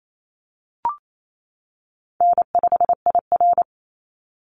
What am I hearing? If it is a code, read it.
Morse code, heard as a keyed tone, about 28 wpm, 700 Hz. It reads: N5IR
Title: Easy Call Signs - 28wpm